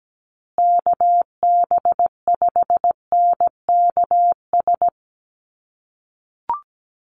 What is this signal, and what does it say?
Morse code, heard as a keyed tone, about 17 wpm, 700 Hz. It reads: KB5NKS
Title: USA Call Signs Repeated 3x 17wpm